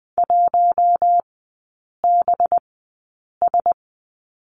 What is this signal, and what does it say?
Morse code, heard as a keyed tone, about 20 wpm, spaced as for 8 wpm, 700 Hz. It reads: 1BS
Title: 184 Lesson 30e - Letter b - Three Characters 20wpm